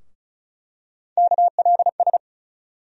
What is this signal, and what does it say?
Morse code, heard as a keyed tone, about 34 wpm, 700 Hz. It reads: KLS
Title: USA Call Sign Postfixes - 3 Letters 35wpm